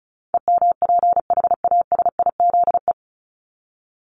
Morse code, heard as a keyed tone, about 34 wpm, 700 Hz. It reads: EMPHASIZE
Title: Top 1251-1500 Words 35wpm